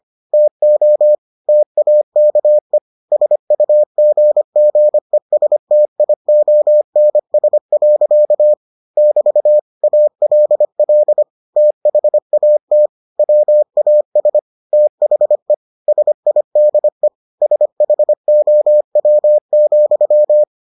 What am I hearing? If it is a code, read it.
Morse code, heard as a keyed tone, about 25 wpm, 600 Hz. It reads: TO TAKE SUGGESTIONS. <BT> ALL THAT WAS THE SIDE SHOW,